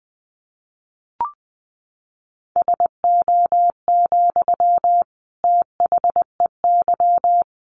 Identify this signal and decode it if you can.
Morse code, heard as a keyed tone, about 20 wpm, 700 Hz. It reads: SO, THEY
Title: Sentences from Top 1400 Words 20wpm